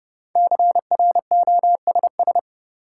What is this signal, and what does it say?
Morse code, heard as a keyed tone, about 32 wpm, 700 Hz. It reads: CROSS
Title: Sentences from Top 300 Words - New Word Drill - 30wpm